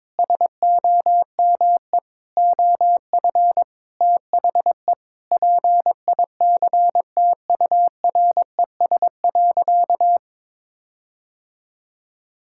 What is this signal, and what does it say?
Morse code, heard as a keyed tone, about 22 wpm, 700 Hz. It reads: SOME OF THE PICTURES.
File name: Sentences with 7-Word Phrases - 22wpm